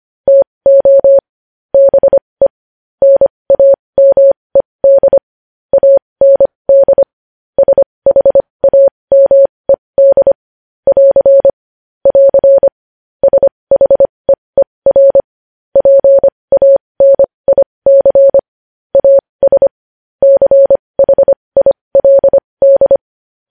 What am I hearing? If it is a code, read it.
Morse code, heard as a keyed tone, about 25 wpm, spaced as for 20 wpm, 550 Hz. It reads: TO BE NAMED AND SHAMED <AR> <AR> SHEER PANIC AS CHILD
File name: podcast / News headlines-1x-25x20